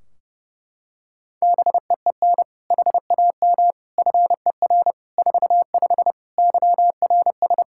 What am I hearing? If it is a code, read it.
Morse code, heard as a keyed tone, about 30 wpm, 700 Hz. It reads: BEEN HAM FER 45 YRS
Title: Practice QSO - Exchange 3 - More Info - 30wpm